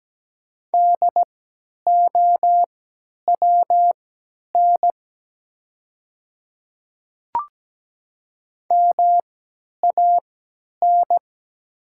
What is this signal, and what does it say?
Morse code, heard as a keyed tone, about 17 wpm, spaced as for 10 wpm, 700 Hz. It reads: DOWN MAN
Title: 3x Character Spacing - Top 300 Words - 17wpm